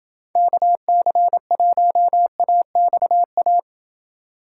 Morse code, heard as a keyed tone, about 29 wpm, 700 Hz. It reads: KC1AXA